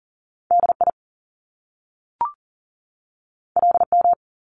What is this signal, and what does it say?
Morse code, heard as a keyed tone, about 38 wpm, 700 Hz. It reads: DI LM
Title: Top Two Letter Combinations in Words 40wpm